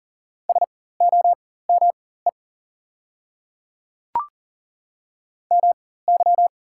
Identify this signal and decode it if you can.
Morse code, heard as a keyed tone, about 39 wpm, spaced as for 18 wpm, 700 Hz. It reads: SOME MY
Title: 4x Character Spacing - Top 100 Words - 40wpm